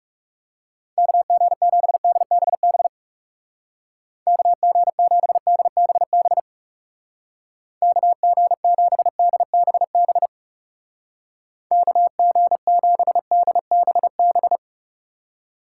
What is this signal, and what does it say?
Morse code, heard as a keyed tone, about 39 wpm, 700 Hz. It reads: KG7DBB KG7DBB KG7DBB KG7DBB